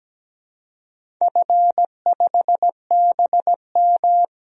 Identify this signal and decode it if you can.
Morse code, heard as a keyed tone, about 17 wpm, 700 Hz. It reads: F5BM
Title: DXCC-100 Call Signs - 4 Characters - 17wpm